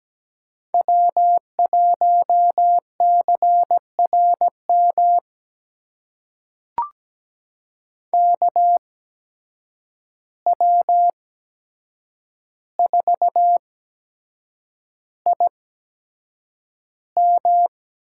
Morse code, heard as a keyed tone, about 17 wpm, spaced as for 7 wpm, 700 Hz. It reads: W1CRM KW4IM
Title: USA Call Signs - 5 Characters - 8x Character Spacing - 17wpm